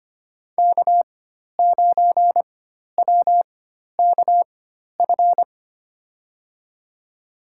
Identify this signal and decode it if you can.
Morse code, heard as a keyed tone, about 25 wpm, spaced as for 12 wpm, 700 Hz. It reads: K9WKF
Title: USA Call Signs - 5 Characters - 4x Character Spacing - 25wpm